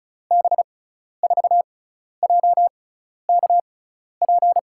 Morse code, heard as a keyed tone, about 36 wpm, spaced as for 12 wpm, 700 Hz. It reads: B4JKP
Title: USA Call Signs - 6 Characters - 6x Character Spacing - 35wpm